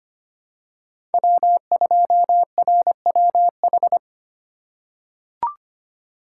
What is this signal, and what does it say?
Morse code, heard as a keyed tone, about 25 wpm, 700 Hz. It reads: W2RWH
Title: USA Call Signs - 5 Characters - 3x Character Spacing - 25wpm